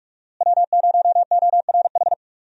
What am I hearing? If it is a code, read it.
Morse code, heard as a keyed tone, about 42 wpm, 700 Hz. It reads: W0ORH
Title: USA Call Signs - 5 Characters - Repeated 3x - 45wpm